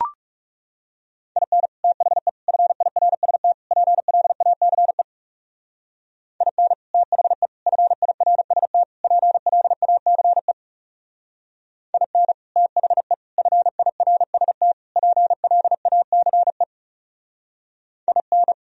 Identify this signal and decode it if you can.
Morse code, heard as a keyed tone, about 38 wpm, 700 Hz. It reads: IN THE FIRST PLACE IN THE FIRST PLACE IN THE FIRST PLACE IN